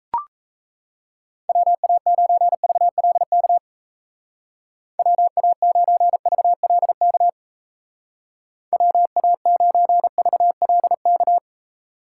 Morse code, heard as a keyed tone, about 34 wpm, 700 Hz. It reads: WA9VLK WA9VLK WA9VLK
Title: Speed Racing - USA 2x3 Call Signs 28wpm